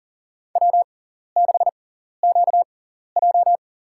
Morse code, heard as a keyed tone, about 40 wpm, spaced as for 14 wpm, 700 Hz. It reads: W6QJ
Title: USA Call Signs - 4 Characters - 6x Character Spacing - 40wpm